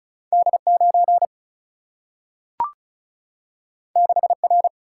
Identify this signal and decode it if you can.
Morse code, heard as a keyed tone, about 36 wpm, 700 Hz. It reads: D 9 6 R
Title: Single-Letter Single-Number 35wpm